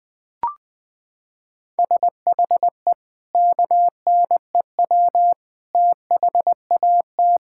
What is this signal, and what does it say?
Morse code, heard as a keyed tone, about 21 wpm, 700 Hz. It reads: SHE KNEW THAT
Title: Sentences with 4-Word Phrases - 20wpm